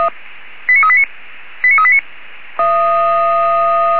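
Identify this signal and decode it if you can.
Five-tone selective call (ZVEI), heard as quick runs of five tones, tones of about 80 ms. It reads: 89389 89389
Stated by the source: Received by SDR, ZVEI five-tone.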